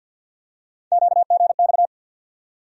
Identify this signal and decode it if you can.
Morse code, heard as a keyed tone, about 50 wpm, 700 Hz. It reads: QGX
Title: USA Call Sign Postfixes - 3 Letters 50wpm